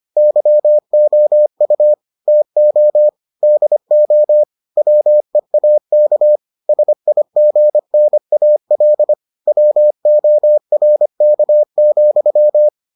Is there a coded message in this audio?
Yes: Morse code from 0.2 to 12.7 s, heard as a keyed tone, about 25 wpm, 600 Hz. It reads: YOU TO DO WEAK SIGNAL WORK,